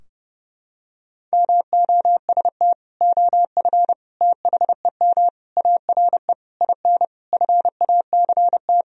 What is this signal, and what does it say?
Morse code, heard as a keyed tone, about 30 wpm, 700 Hz. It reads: MOST OF THEM ARE IN FACT
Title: Sentences from Top 1000 Words - Review Part 4 - 30wpm